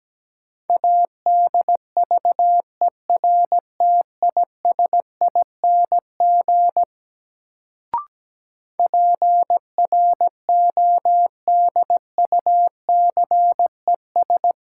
Morse code, heard as a keyed tone, about 17 wpm, 700 Hz. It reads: ADVERTISING PRODUCES
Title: Sentences from Top 1000 Words - Review Part 6 - New Word Drill - 17wpm